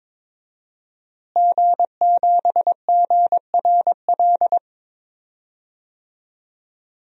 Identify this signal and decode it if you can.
Morse code, heard as a keyed tone, about 22 wpm, 700 Hz. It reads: G7GRL
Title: DXCC-100 Call Signs - 5 Characters - Repeated 2x - 22wpm